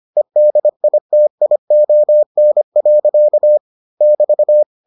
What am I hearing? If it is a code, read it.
Morse code, heard as a keyed tone, about 25 wpm, 600 Hz. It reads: EDITION. <BT>